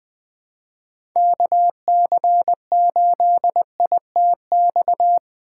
Morse code, heard as a keyed tone, about 20 wpm, 700 Hz. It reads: KC8ITX
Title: USA Call Signs - 6 Characters - 4x Character Spacing - 20wpm